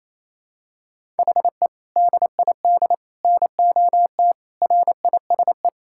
Morse code, heard as a keyed tone, about 27 wpm, 700 Hz. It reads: HE DID NOT RISE
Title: Sentences from Top 1000 Words - Review Part 6 - 28wpm